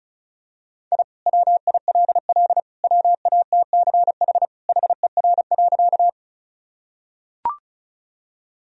Morse code, heard as a keyed tone, about 36 wpm, 700 Hz. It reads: I WILL WATCH HER.
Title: Sentences from Top 1000 Words - Review Part 1 - 35wpm